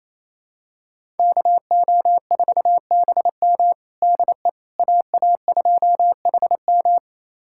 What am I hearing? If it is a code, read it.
Morse code, heard as a keyed tone, about 28 wpm, 700 Hz. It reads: KO4BM DE AA2HM
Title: USA Callsign DE Callsign - 5 Characters - 28wpm